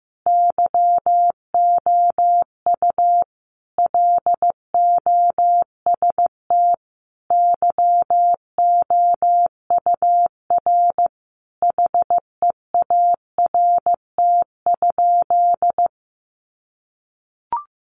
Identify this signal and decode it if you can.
Morse code, heard as a keyed tone, about 15 wpm, 700 Hz. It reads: YOU LOST YOUR HEART?
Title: Sentences from Top 1400 Words 15wpm